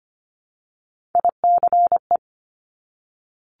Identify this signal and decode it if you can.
Morse code, heard as a keyed tone, about 25 wpm, 700 Hz. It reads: ICE
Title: Multi-letter Phonemes Repeated 3x 25wpm